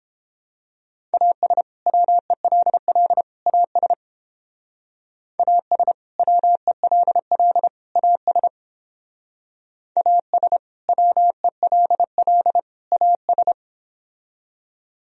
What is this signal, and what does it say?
Morse code, heard as a keyed tone, about 29 wpm, 700 Hz. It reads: AS WELL AS AS WELL AS AS WELL AS